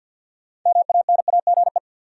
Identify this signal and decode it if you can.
Morse code, heard as a keyed tone, about 50 wpm, 700 Hz. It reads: MANAGE
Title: Sentences from Top 1400 Words - New Word Drill - 50wpm